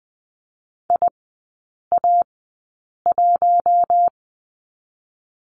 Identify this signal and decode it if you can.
Morse code, heard as a keyed tone, about 20 wpm, spaced as for 8 wpm, 700 Hz. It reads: IA1
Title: 053 Lesson 10e - Letter r - Three Characters 20wpm